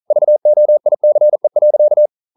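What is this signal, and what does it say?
Morse code, heard as a keyed tone, about 42 wpm, 600 Hz. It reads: VOICE.
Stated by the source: Morse code practice at 41 wpm, from CW med Gnister.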